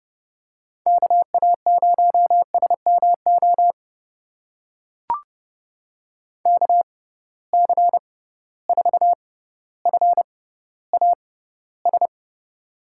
Morse code, heard as a keyed tone, about 31 wpm, spaced as for 15 wpm, 700 Hz. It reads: KA0SMO KC4FAS
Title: USA Call Signs - 6 Characters - 6x Character Spacing - 30wpm